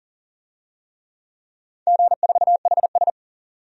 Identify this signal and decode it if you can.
Morse code, heard as a keyed tone, about 37 wpm, 700 Hz. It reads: G4HS
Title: DXCC-100 Call Signs - 4 Characters - 40wpm